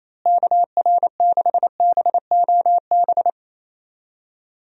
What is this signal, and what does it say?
Morse code, heard as a keyed tone, about 28 wpm, 700 Hz. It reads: KR6BOB
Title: USA Call Signs - 6 Characters - Repeated 3x - 28wpm